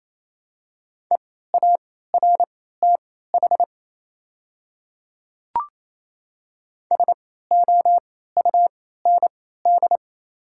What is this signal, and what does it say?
Morse code, heard as a keyed tone, about 28 wpm, spaced as for 16 wpm, 690 Hz. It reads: EARTH SOUND
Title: 3x Character Spacing - Top 500 Words - 28wpm